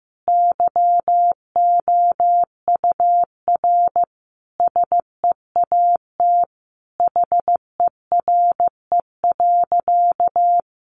Morse code, heard as a keyed tone, about 15 wpm, 700 Hz. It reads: YOUR SEAT HERE.
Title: Sentences from Top 1500 Words 15wpm